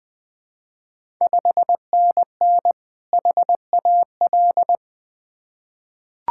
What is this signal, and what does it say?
Morse code, heard as a keyed tone, about 20 wpm, 700 Hz. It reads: 5NN HAL
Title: Ontario QSO Party Exchange 20wpm